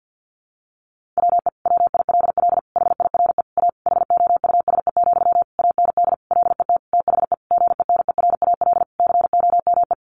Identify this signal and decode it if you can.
Morse code, heard as a keyed tone, about 48 wpm, 700 Hz. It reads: WE WILL HIRE A HOUSE, AND LET THE GENERAL COME
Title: Sentences from Top 1400 Words 50wpm